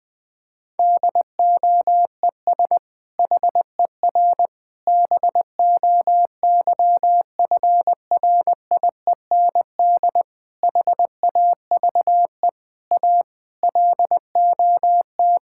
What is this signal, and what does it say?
Morse code, heard as a keyed tone, about 21 wpm, 700 Hz. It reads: DOES HER BOYFRIEND HAVE A LOT
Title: Sentences from Top 1000 Words - Review Part 5 - 20wpm